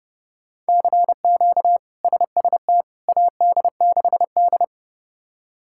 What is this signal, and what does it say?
Morse code, heard as a keyed tone, about 32 wpm, 700 Hz. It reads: CQ SST AD6D
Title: K1USN SST QSO Practice - 30wpm